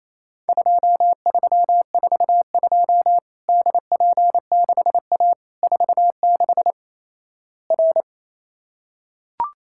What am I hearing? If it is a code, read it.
Morse code, heard as a keyed tone, about 28 wpm, 700 Hz. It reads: 2342 DP6A 46 R
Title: Worked All Europe DX Contest - QTC Exchange - 28wpm